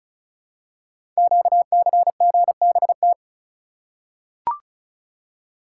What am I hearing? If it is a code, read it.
Morse code, heard as a keyed tone, about 36 wpm, 690 Hz. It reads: QCGBT
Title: Code Groups - 5 Letter:Numbers - 35wpm